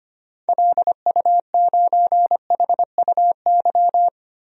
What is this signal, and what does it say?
Morse code, heard as a keyed tone, about 25 wpm, 700 Hz. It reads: LU9HUY
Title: DXCC-100 Call Signs - 6 Characters - 25wpm